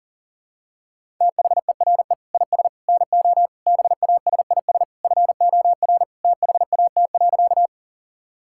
Morse code, heard as a keyed tone, about 40 wpm, 700 Hz. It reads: THERE IS NO BASIS FOR THAT.
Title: Sentences from Top 1000 Words - Review Part 5 - 40wpm